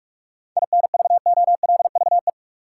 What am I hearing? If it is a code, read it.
Morse code, heard as a keyed tone, about 44 wpm, 700 Hz. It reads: INVOLVE
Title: Seven Letter Words - 45wpm